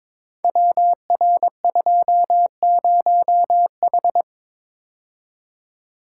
Morse code, heard as a keyed tone, about 22 wpm, 700 Hz. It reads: WR20H
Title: Code Groups - 5 Letter:Numbers - 22wpm